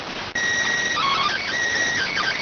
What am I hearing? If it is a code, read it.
SSTV (slow-scan television), PD 290.